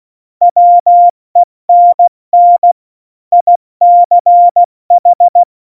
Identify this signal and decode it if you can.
Morse code, heard as a keyed tone, about 16 wpm, 700 Hz. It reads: WENN ICH
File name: Morsekurs / Witze / Witz 14. Januar 2020, 70 F80